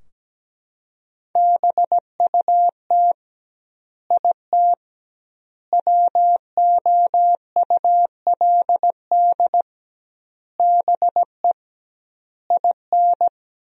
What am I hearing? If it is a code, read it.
Morse code, heard as a keyed tone, about 17 wpm, 700 Hz. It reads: BUT IT WOULD BE IN
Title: 2x Word-Spacing - Sentences from Top 300 Words - 17wpm